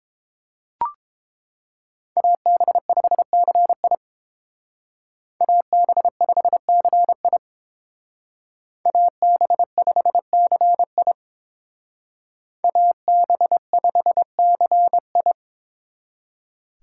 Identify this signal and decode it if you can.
Morse code, heard as a keyed tone, about 31 wpm, 700 Hz. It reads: AB5CI AB5CI AB5CI AB5CI